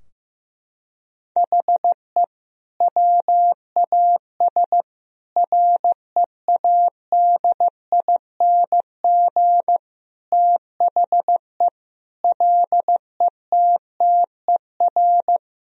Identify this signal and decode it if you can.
Morse code, heard as a keyed tone, about 15 wpm, 700 Hz. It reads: HE WAS READING THE LETTER
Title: Sentences from Top 1000 Words - Review Part 3 - 15wpm